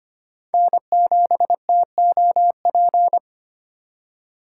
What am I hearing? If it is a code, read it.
Morse code, heard as a keyed tone, about 25 wpm, 700 Hz. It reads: N7TOP
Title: USA Call Signs - 4x Character Spacing - 25wpm